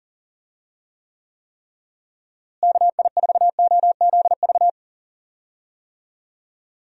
Morse code, heard as a keyed tone, about 40 wpm, 700 Hz. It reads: KI4OZV
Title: USA Call Signs - 6 Characters - Repeated 3x - 40wpm